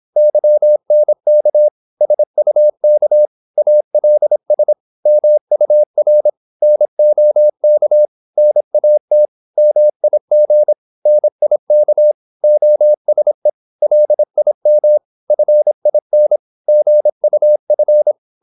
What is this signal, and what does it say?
Morse code, heard as a keyed tone, about 26 wpm, 600 Hz. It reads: YNK SUK ALS MUR NOK NAT MIG NIK OSE LIM FIN GUF
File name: Korte ord DK 26wpm